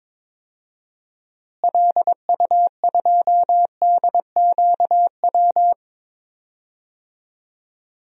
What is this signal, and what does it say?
Morse code, heard as a keyed tone, about 21 wpm, 700 Hz. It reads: LU2DQW
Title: DXCC-100 Call Signs - 6 Characters - Repeated 3x - 22wpm